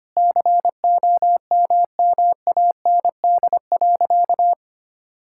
Morse code, heard as a keyed tone, about 25 wpm, 700 Hz. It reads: COMMAND.
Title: Sentences from Top 1600 Words 25wpm